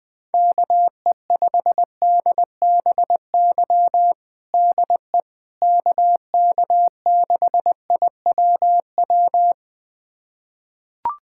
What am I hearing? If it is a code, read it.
Morse code, heard as a keyed tone, about 20 wpm, 700 Hz. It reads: KE5DBY DE KK6IWW
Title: USA Callsign DE Callsign - 6 Characters - 20wpm